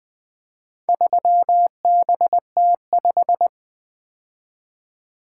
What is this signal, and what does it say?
Morse code, heard as a keyed tone, about 20 wpm, 700 Hz. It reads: 3BT5
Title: Four Letter-Number Combinations - 20wpm